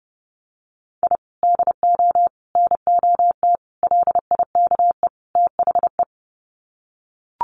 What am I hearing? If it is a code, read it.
Morse code, heard as a keyed tone, about 30 wpm, 700 Hz. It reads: I DO NOT LIKE THE
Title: Sets of 5 Words Encore 30wpm